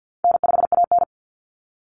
Morse code, heard as a keyed tone, about 50 wpm, 700 Hz. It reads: N5AN
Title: Call Signs Repeated 3x - 50wpm